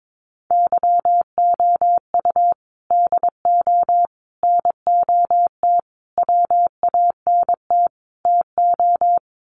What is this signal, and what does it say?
Morse code, heard as a keyed tone, about 22 wpm, 700 Hz. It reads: YOU DO NOT WANT TO
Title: Sets of 5 Words 22wpm